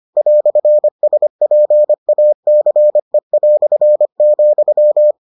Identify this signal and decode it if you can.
Morse code, heard as a keyed tone, about 25 wpm, 600 Hz. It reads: "SPACE",